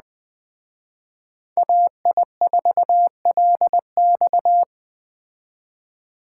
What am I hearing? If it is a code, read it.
Morse code, heard as a keyed tone, about 20 wpm, 700 Hz. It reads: AI4LX
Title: Speed Racing - USA 2x2 Call Signs 17wpm